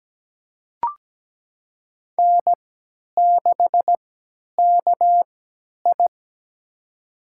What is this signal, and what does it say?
Morse code, heard as a keyed tone, about 17 wpm, spaced as for 10 wpm, 700 Hz. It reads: N6KI
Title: USA Call Signs - 3x Character Spacing - 17wpm